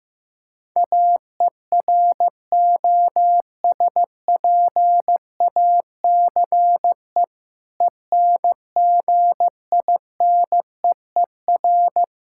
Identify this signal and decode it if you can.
Morse code, heard as a keyed tone, about 15 wpm, 700 Hz. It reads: AEROSPACE ENGINEER
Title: Practice QSO - Exchange 3 - More Info - 15wpm